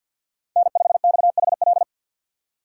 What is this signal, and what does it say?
Morse code, heard as a keyed tone, about 53 wpm, 700 Hz. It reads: N5XHL
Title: USA Call Signs - 5 Characters - 3x Character Spacing - 50wpm